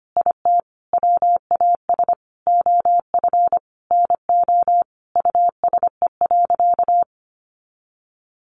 Morse code, heard as a keyed tone, about 25 wpm, 700 Hz. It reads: IT WAS OF NO USE.
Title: Sentences from Top 100 Words 25wpm